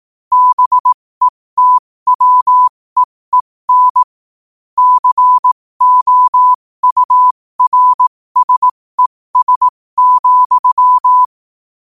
Morse code, heard as a keyed tone, about 18 wpm, spaced as for 15 wpm, 1000 Hz. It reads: BETWEEN COURSES,
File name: qotdfiles / 15wpm